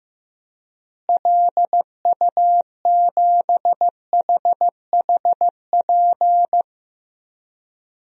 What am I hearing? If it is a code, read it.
Morse code, heard as a keyed tone, about 15 wpm, 700 Hz. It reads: LU7HHP
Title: DXCC-100 Call Signs - 6 Characters - Repeated 2x - 15wpm